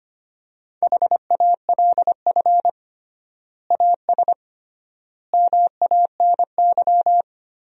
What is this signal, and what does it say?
Morse code, heard as a keyed tone, about 25 wpm, 700 Hz. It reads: HALF AS MANY
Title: 3x Word-Spacing - Sentences from Top 400 Words - 25wpm